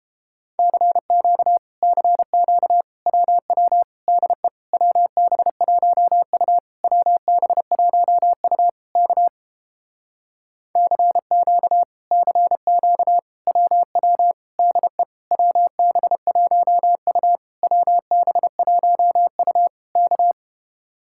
Morse code, heard as a keyed tone, about 31 wpm, 700 Hz. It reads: CQ CQ WW DE WB1U WB1U K CQ CQ WW DE WB1U WB1U K